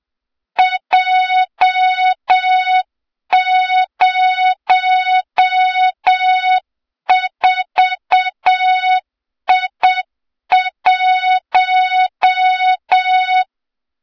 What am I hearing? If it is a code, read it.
Morse code, heard as a keyed tone, about 7 wpm, 3700 Hz, with noise. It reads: J04I1
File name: Morze-HA2MN / hang / 23 GYAKORLAT